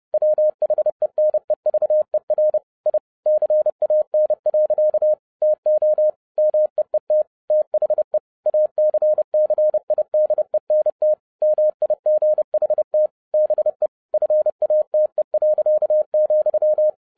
Morse code, heard as a keyed tone, about 30 wpm, 610 Hz. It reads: WHENEVER I CAN. TO MEET THE ACCIDENT MIGHT BE FATE.,
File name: 2011-07-mp3 / n60 BACK FROM A BIG ACCIDENT 30wpm